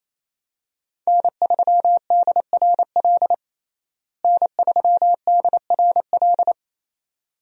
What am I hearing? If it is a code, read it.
Morse code, heard as a keyed tone, about 28 wpm, 700 Hz. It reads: N3DRL N3DRL
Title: USA Call Signs - 5 Characters - Repeated 2x - 28wpm